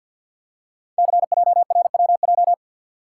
Morse code, heard as a keyed tone, about 48 wpm, 690 Hz. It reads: CJRWJ
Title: Code Groups - 5 Letters - 50wpm